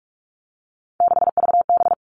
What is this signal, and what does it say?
Morse code, heard as a keyed tone, about 46 wpm, 700 Hz. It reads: 6VB
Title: Three Challenging Letter-Number Combinations 45wpm